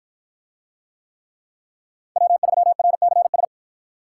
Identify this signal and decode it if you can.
Morse code, heard as a keyed tone, about 48 wpm, 700 Hz. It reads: W3RCS